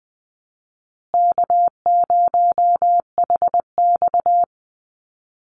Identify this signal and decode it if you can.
Morse code, heard as a keyed tone, about 19 wpm, 700 Hz. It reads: K0HX
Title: Easy Call Signs Repeated 2x - 20wpm